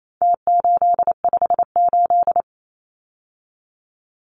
Morse code, heard as a keyed tone, about 28 wpm, 700 Hz. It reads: T858
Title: Numbers - Contest Style - 4-Digit 28wpm